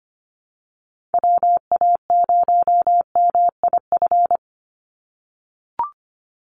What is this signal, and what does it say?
Morse code, heard as a keyed tone, about 25 wpm, 700 Hz. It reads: WA0MIF
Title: USA Call Signs 25wpm